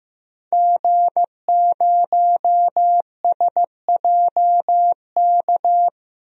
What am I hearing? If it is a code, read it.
Morse code, heard as a keyed tone, about 15 wpm, 700 Hz. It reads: G0SJK
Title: DXCC-100 Call Signs - 5 Characters - 15wpm